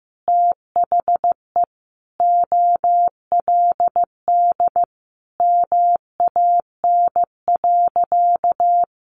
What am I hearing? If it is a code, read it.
Morse code, heard as a keyed tone, about 15 wpm, 700 Hz. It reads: THE OLD MAN.
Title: Sentences from Top 1200 Words 15wpm